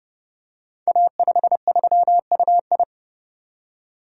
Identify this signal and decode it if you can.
Morse code, heard as a keyed tone, about 28 wpm, 700 Hz. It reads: A53UI
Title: Code Groups - 5 Letter:Numbers - 30wpm